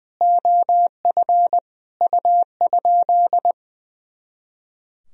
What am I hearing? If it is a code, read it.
Morse code, heard as a keyed tone, about 21 wpm, 700 Hz. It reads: OF U?
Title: Sentences from Top 300 Words and Abbreviations 20wpm